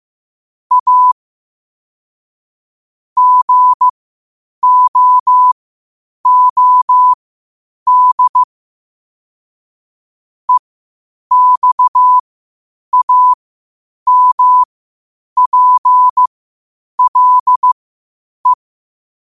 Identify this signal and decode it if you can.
Morse code, heard as a keyed tone, about 15 wpm, spaced as for 8 wpm, 1000 Hz. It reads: A GOOD EXAMPLE
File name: qotdfiles / 8wpm